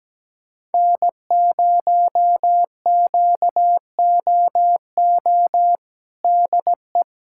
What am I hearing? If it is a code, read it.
Morse code, heard as a keyed tone, about 17 wpm, 700 Hz. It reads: N0QOO DE
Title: USA Callsign DE Callsign - 5 Characters - 17wpm